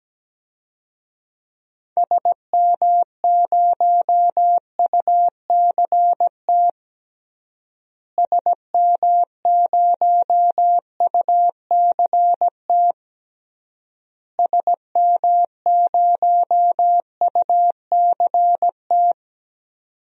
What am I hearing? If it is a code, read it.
Morse code, heard as a keyed tone, about 17 wpm, 700 Hz. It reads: SM0UCT SM0UCT SM0UCT
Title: DXCC-100 Call Signs - 6 Characters - Repeated 3x - 17wpm